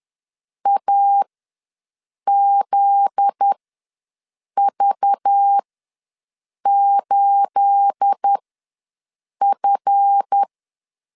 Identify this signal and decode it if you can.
Morse code, heard as a keyed tone, about 10 wpm, spaced as for 6 wpm, 800 Hz. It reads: AZV8F
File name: LEZIONI / 9 28 ws 10wpm 4124